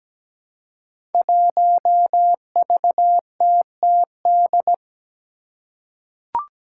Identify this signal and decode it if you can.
Morse code, heard as a keyed tone, about 17 wpm, 700 Hz. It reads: 1VTTD
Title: Five Challenging Letter-Number Combinations 17wpm